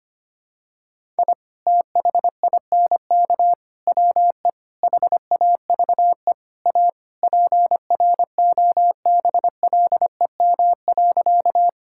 Morse code, heard as a keyed tone, about 25 wpm, 700 Hz. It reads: I THINK WE HAVE A PROBLEM.
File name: practice / 25wpm / Sentences with 4-Word Phrases - 25wpm